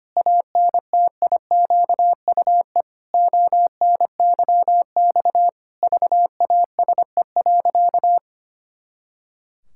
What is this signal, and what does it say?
Morse code, heard as a keyed tone, about 25 wpm, 700 Hz. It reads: ANTIQUE ONYX VASE.
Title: Pangram Challenge v2 - 25wpm